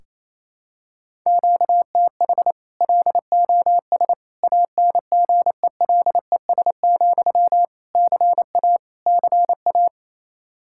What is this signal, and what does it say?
Morse code, heard as a keyed tone, about 28 wpm, 700 Hz. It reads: QTH LOS ANGELES, CA CA
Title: Practice QSO - Exchange 1 - Beginning - 28wpm